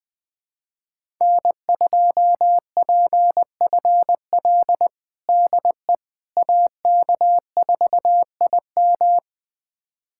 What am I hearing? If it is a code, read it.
Morse code, heard as a keyed tone, about 20 wpm, 700 Hz. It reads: N2PFL DE AK4IM
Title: USA Callsign DE Callsign - 5 Characters - 20wpm